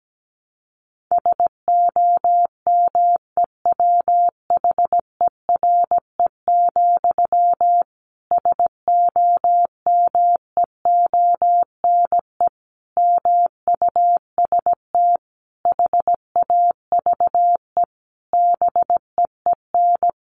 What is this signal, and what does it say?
Morse code, heard as a keyed tone, about 17 wpm, 700 Hz. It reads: SOMEWHERE, SOMEONE MUST HAVE BEEN
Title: A Sound Of Thunder 17wpm